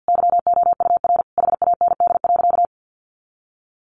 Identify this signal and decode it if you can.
Morse code, heard as a keyed tone, about 50 wpm, 700 Hz. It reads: YOUR HAND.
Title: Sentences from Top 400 Words 50wpm